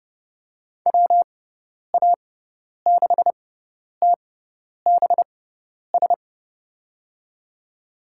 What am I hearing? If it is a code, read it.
Morse code, heard as a keyed tone, about 30 wpm, spaced as for 10 wpm, 700 Hz. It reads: WA6TBS
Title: USA Call Signs - 6 Characters - 6x Character Spacing - 30wpm